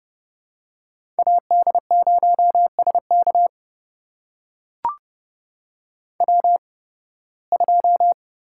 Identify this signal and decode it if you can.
Morse code, heard as a keyed tone, about 30 wpm, spaced as for 20 wpm, 700 Hz. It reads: AD0SK W2
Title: USA Call Signs - 5 Characters - 8x Character Spacing - 30wpm